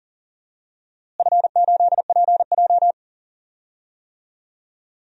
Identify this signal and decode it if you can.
Morse code, heard as a keyed tone, about 40 wpm, 700 Hz. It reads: F8PJ
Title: DXCC-100 Call Signs - 4 Characters - Repeated 2x - 40wpm